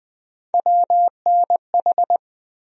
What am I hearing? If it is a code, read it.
Morse code, heard as a keyed tone, about 20 wpm, 700 Hz. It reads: WNH
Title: USA Call Sign Postfixes - 3 Letters 20wpm